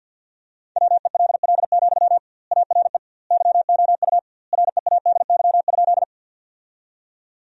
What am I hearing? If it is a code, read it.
Morse code, heard as a keyed tone, about 50 wpm, 700 Hz. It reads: WELL, ARE YOU READY?
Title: 2x Word-Spacing - Sentences from Top 400 Words - 50wpm